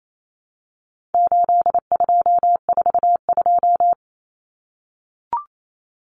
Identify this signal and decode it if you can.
Morse code, heard as a keyed tone, about 28 wpm, 700 Hz. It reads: 8242
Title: Common Cut Numbers - 4-Digit 28wpm